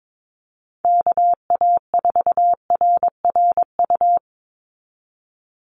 Morse code, heard as a keyed tone, about 22 wpm, 700 Hz. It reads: KA4RRU
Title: Call Signs Repeated 3x - 22wpm